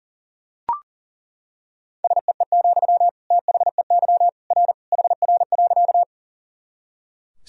Morse code, heard as a keyed tone, about 37 wpm, 700 Hz. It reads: SEE, THEY R HR.
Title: Sentences from Top 200 Words and Abbreviations 40wpm